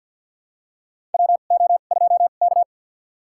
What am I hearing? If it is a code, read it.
Morse code, heard as a keyed tone, about 51 wpm, spaced as for 36 wpm, 700 Hz. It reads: WO2K
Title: USA Call Signs - 4 Characters - 2x Character Spacing - 50wpm